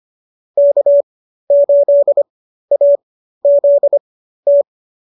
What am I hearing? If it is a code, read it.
Morse code, heard as a keyed tone, about 25 wpm, spaced as for 13 wpm, 590 Hz. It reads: K8AZT